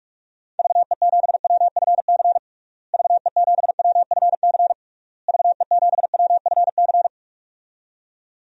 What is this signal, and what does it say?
Morse code, heard as a keyed tone, about 46 wpm, 700 Hz. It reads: VE7WFC VE7WFC VE7WFC
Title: DXCC-100 Call Signs - 6 Characters - Repeated 3x - 45wpm